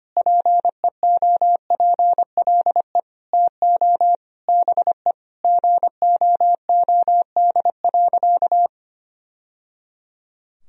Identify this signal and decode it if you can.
Morse code, heard as a keyed tone, about 25 wpm, 700 Hz. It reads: PEOPLE TO BE GOOD.
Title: Sentences from Top 1000 Words - Review Part 3 - 25wpm